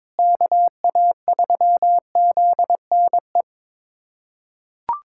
Morse code, heard as a keyed tone, about 22 wpm, 700 Hz. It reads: KA3ZNE
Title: USA Callsign DE Callsign - 6 Characters - 22wpm